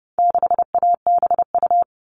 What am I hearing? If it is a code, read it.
Morse code, heard as a keyed tone, about 30 wpm, 700 Hz. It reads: 6ABU
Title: All Cut Numbers - 4-Digit 30wpm